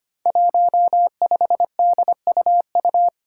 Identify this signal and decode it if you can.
Morse code, heard as a keyed tone, about 25 wpm, 700 Hz. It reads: 15DUU
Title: Five Challenging Letter-Number Combinations 25wpm